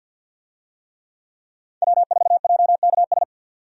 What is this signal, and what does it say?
Morse code, heard as a keyed tone, about 52 wpm, 700 Hz. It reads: W4JKS
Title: USA Call Signs - 5 Characters - 50wpm